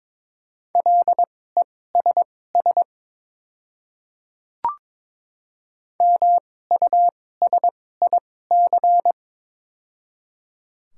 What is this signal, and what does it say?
Morse code, heard as a keyed tone, about 22 wpm, spaced as for 16 wpm, 700 Hz. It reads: LESS MUSIC
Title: 2x Character Spacing - Top 500 Words - 22wpm